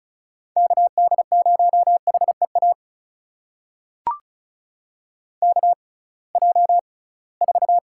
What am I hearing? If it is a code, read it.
Morse code, heard as a keyed tone, about 36 wpm, spaced as for 23 wpm, 700 Hz. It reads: KD0HEA KJ4
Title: USA Call Signs - 6 Characters - 6x Character Spacing - 35wpm